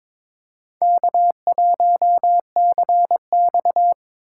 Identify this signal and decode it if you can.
Morse code, heard as a keyed tone, about 22 wpm, 700 Hz. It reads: K1CX
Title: K1USN SST QSO Practice - 22wpm